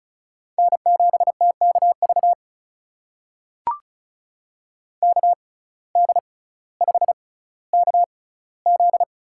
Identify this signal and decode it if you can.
Morse code, heard as a keyed tone, about 36 wpm, spaced as for 18 wpm, 700 Hz. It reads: N7TKV KD5KZ
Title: USA Call Signs - 5 Characters - 6x Character Spacing - 35wpm